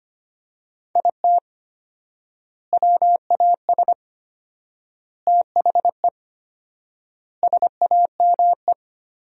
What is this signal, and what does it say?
Morse code, heard as a keyed tone, about 25 wpm, 700 Hz. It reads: IT WAS THE SAME
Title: 4x Word-Spacing - Sentences from Top 500 Words - 25wpm